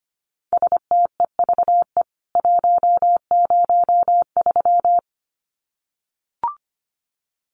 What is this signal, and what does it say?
Morse code, heard as a keyed tone, about 25 wpm, 700 Hz. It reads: STEVE 103
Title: CWT Exchange 25wpm